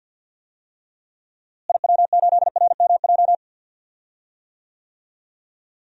Morse code, heard as a keyed tone, about 52 wpm, 700 Hz. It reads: IW8RMJ
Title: DXCC-100 Call Signs - 6 Characters - Repeated 2x - 50wpm